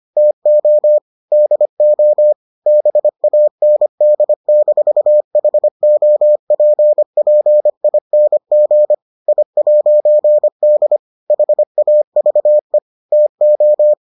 Morse code, heard as a keyed tone, about 25 wpm, 600 Hz. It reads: TO DO BAND-HOPPING I'D HAVE TO